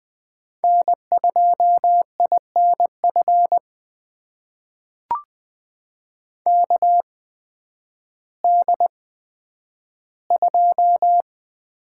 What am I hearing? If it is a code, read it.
Morse code, heard as a keyed tone, about 20 wpm, spaced as for 11 wpm, 700 Hz. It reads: N2INF KD2
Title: USA Call Signs - 8x Character Spacing - 20wpm